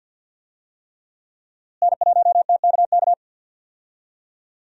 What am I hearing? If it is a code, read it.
Morse code, heard as a keyed tone, about 52 wpm, 700 Hz. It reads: N1TKK
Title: USA Call Signs - 5 Characters - Repeated 3x - 50wpm